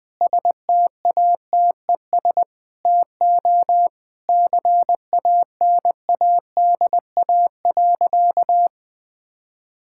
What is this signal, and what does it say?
Morse code, heard as a keyed tone, about 20 wpm, 700 Hz. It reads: STATES TO CANADA.
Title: Sentences with 4-Word Phrases - 20wpm